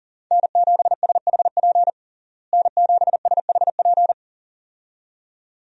Morse code, heard as a keyed tone, about 40 wpm, 700 Hz. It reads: N7SHP N7SHP
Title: USA Call Signs - 5 Characters - Repeated 2x - 40wpm